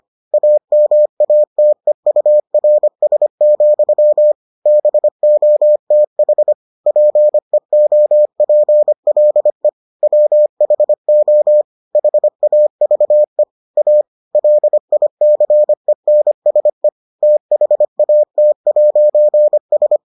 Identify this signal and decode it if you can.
Morse code, heard as a keyed tone, about 25 wpm, 600 Hz. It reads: AMATEURS, BOTH PEOPLE WHO HAVE A LICENSE THAT'S